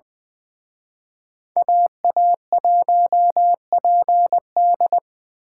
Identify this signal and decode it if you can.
Morse code, heard as a keyed tone, about 25 wpm, 700 Hz. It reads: AA1PD
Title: Speed Racing - USA 2x2 Call Signs 20wpm